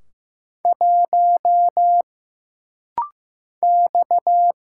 Morse code, heard as a keyed tone, about 15 wpm, 700 Hz. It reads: 1 X
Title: Single Letter-Number Alt - Rapid-Fire 15wpm